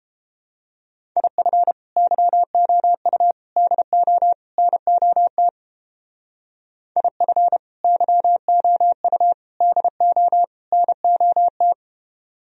Morse code, heard as a keyed tone, about 29 wpm, 700 Hz. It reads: IF YOU DO NOT IF YOU DO NOT